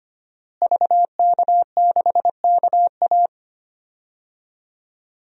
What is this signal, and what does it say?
Morse code, heard as a keyed tone, about 25 wpm, 700 Hz. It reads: VK6KA
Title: DXCC-100 Call Signs - 5 Characters - Repeated 2x - 25wpm